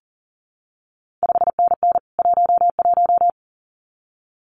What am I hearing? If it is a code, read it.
Morse code, heard as a keyed tone, about 41 wpm, 700 Hz. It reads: 5NN 11
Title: CQ WW DX Contest Exchange Encore - 40wpm